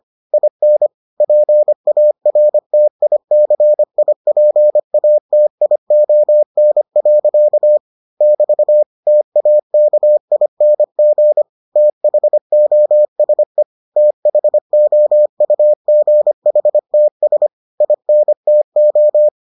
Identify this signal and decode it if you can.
Morse code, heard as a keyed tone, about 25 wpm, 600 Hz. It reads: IN PARTICIPATION. <BT> TAKING THOSE THOUGHTS INTO